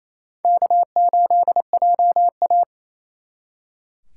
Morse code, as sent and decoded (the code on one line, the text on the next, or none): -.- ---.. .--- .-
K8JA